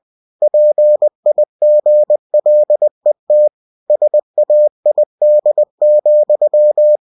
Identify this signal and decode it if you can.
Morse code, heard as a keyed tone, about 20 wpm, 600 Hz. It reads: PIGLET SAID,
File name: cw-books / winnie-the-pooh / winnie-20wpm-page-0038